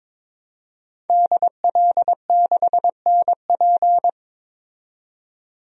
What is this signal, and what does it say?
Morse code, heard as a keyed tone, about 22 wpm, 700 Hz. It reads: DL6NP
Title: DXCC-100 Call Signs - 5 Characters - 22wpm